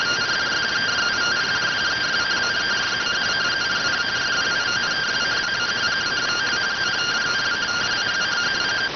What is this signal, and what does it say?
Two-tone teleprinter signal, about 170 Hz shift.